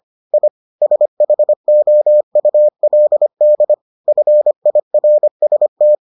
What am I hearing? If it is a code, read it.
Morse code, heard as a keyed tone, about 25 wpm, 600 Hz. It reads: I SHOULD FIRST